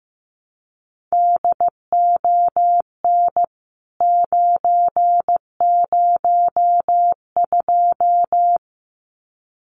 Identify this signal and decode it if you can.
Morse code, heard as a keyed tone, about 15 wpm, 700 Hz. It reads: DON 902
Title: CWT Exchange 15wpm